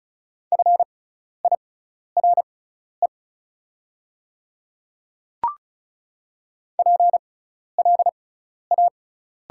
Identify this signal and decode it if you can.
Morse code, heard as a keyed tone, about 36 wpm, spaced as for 12 wpm, 690 Hz. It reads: FIRE PLA
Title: 6x Character Spacing - Top 500 Words - 35wpm